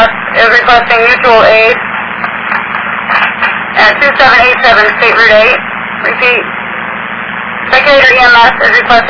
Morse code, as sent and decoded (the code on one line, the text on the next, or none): none